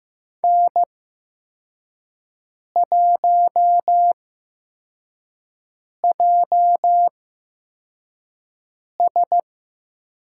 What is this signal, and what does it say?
Morse code, heard as a keyed tone, about 15 wpm, spaced as for 4 wpm, 700 Hz. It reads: N1JS
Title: USA Call Signs - 4 Characters - 8x Character Spacing - 15wpm